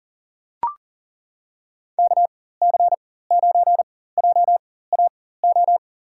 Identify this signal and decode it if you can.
Morse code, heard as a keyed tone, about 38 wpm, spaced as for 18 wpm, 700 Hz. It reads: KC9JAO
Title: USA Call Signs - 6 Characters - 4x Character Spacing - 40wpm